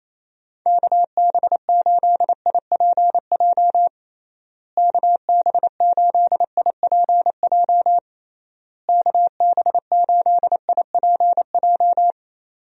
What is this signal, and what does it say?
Morse code, heard as a keyed tone, about 28 wpm, 700 Hz. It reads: KB8IPJ KB8IPJ KB8IPJ
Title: USA Call Signs - 6 Characters - Repeated 3x - 28wpm